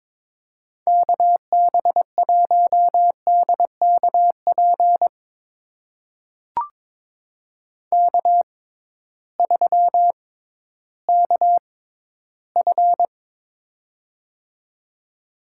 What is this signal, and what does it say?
Morse code, heard as a keyed tone, about 22 wpm, spaced as for 13 wpm, 700 Hz. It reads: KB1DKP K3KF
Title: USA Call Signs - 6x Character Spacing - 22wpm